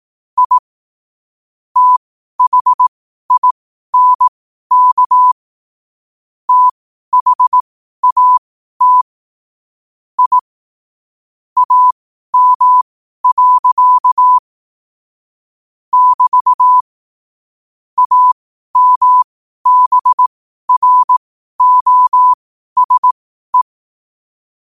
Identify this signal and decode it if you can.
Morse code, heard as a keyed tone, about 18 wpm, spaced as for 12 wpm, 1000 Hz. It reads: I THINK THAT I AM. <BT> AMBROSE
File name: qotdfiles / 12wpm